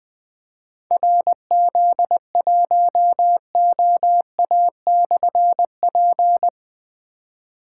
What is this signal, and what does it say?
Morse code, heard as a keyed tone, about 20 wpm, 700 Hz. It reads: RZ1OA/P
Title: Difficult Call Signs 20wpm